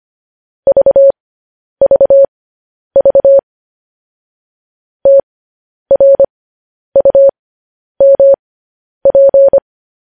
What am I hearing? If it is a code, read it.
Morse code, heard as a keyed tone, about 25 wpm, spaced as for 10 wpm, 550 Hz. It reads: VVV TRUMP
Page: News Headlines in Morse Code at 10 WPM